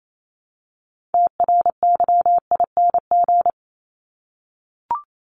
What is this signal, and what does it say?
Morse code, heard as a keyed tone, about 28 wpm, 700 Hz. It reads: TRYING
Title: ING Suffix - Top 100 Words with Base Words 28wpm